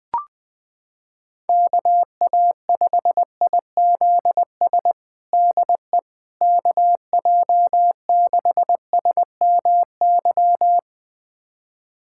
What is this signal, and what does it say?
Morse code, heard as a keyed tone, about 20 wpm, 700 Hz. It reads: KA5IZS DE KJ6SMY
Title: USA Callsign DE Callsign - 6 Characters - 20wpm